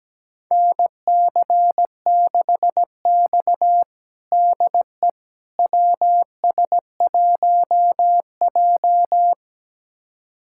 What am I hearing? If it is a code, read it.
Morse code, heard as a keyed tone, about 17 wpm, 700 Hz. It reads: NC6X DE WS1J
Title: USA Callsign DE Callsign - 4 Characters - 17wpm